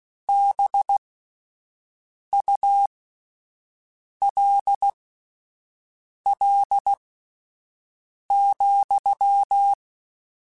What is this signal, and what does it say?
Morse code, heard as a keyed tone, about 16 wpm, spaced as for 5 wpm, 800 Hz. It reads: BULL,